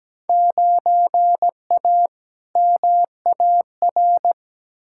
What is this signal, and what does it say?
Morse code, heard as a keyed tone, about 17 wpm, 700 Hz. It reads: 9A MAR
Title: Field Day Exchanges Encore - 17wpm